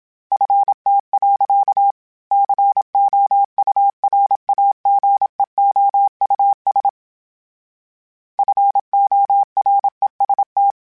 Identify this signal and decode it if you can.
Morse code, heard as a keyed tone, about 27 wpm, 800 Hz. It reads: FT. COURAGEOUS FOREST
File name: fromSMs / FromSMs mp3s / Cities@25